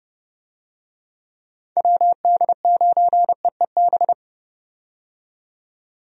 Morse code, heard as a keyed tone, about 30 wpm, 700 Hz. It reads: WD9EEB
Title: USA Call Signs - 6 Characters - 30wpm